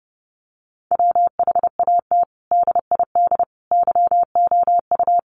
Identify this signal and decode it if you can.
Morse code, heard as a keyed tone, about 30 wpm, 700 Hz. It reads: WHAT DID YOU
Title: Sentences from Top 100 Words 30wpm